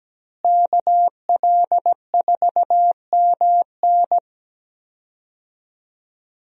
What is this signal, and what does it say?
Morse code, heard as a keyed tone, about 17 wpm, 700 Hz. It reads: KL4MN
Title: USA Call Signs - 5 Characters - Repeated 2x - 17wpm